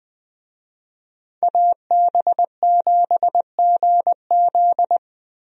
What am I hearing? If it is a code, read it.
Morse code, heard as a keyed tone, about 20 wpm, 700 Hz. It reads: AB7GZ
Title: USA Call Signs - 5 Characters - Repeated 2x - 20wpm